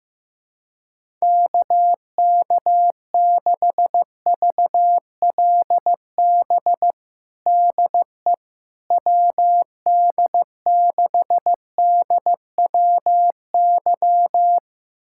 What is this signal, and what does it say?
Morse code, heard as a keyed tone, about 15 wpm, 700 Hz. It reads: KK6VLB DE WD6DWY
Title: USA Callsign DE Callsign - 6 Characters - 15wpm